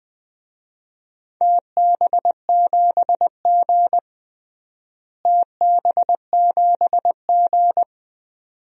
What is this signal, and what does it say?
Morse code, heard as a keyed tone, about 21 wpm, 700 Hz. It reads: TB7G TB7G
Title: DXCC-100 Call Signs - 4 Characters - Repeated 2x - 20wpm